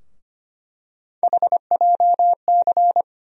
Morse code, heard as a keyed tone, about 25 wpm, 700 Hz. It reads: HJC
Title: USA Call Sign Postfixes - 3 Letters 25wpm